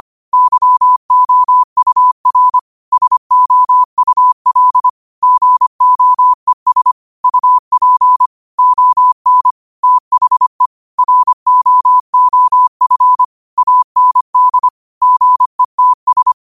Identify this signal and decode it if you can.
Morse code, heard as a keyed tone, about 25 wpm, 1000 Hz. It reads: YOUR SOUL GOES UP ON THE ROOF AND GETS
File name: qotdfiles / 25wpm